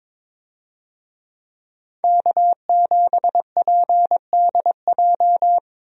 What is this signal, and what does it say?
Morse code, heard as a keyed tone, about 22 wpm, 700 Hz. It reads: K7PDJ
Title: USA Call Signs - 5 Characters - 22wpm